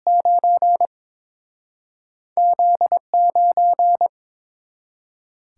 Morse code, heard as a keyed tone, about 32 wpm, spaced as for 6 wpm, 700 Hz. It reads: 9Z 9